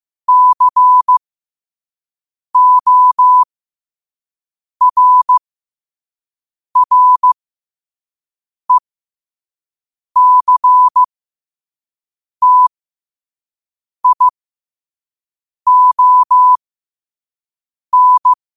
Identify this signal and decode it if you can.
Morse code, heard as a keyed tone, about 15 wpm, spaced as for 5 wpm, 1000 Hz. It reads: CORRECTION
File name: qotdfiles / 5wpm